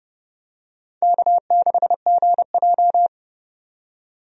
Morse code, heard as a keyed tone, about 30 wpm, 700 Hz. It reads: K6GJ
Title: USA Call Signs - 4 Characters - 30wpm